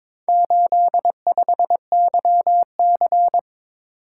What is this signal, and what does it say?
Morse code, heard as a keyed tone, about 22 wpm, 700 Hz. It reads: 85YC
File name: practice / 22wpm / Four Letter-Number Combinations - 22wpm